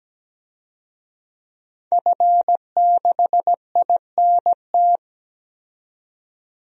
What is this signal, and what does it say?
Morse code, heard as a keyed tone, about 17 wpm, 700 Hz. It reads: F6INT
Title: DXCC-100 Call Signs - 5 Characters - 17wpm